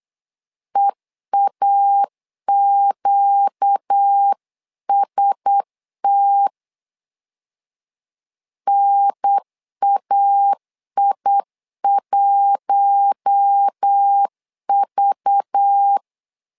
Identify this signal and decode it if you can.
Morse code, heard as a keyed tone, about 8 wpm, 800 Hz. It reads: EAQST NAI1V
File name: LEZIONI / 5 LBVC / 5 LBVC 8wpm 2935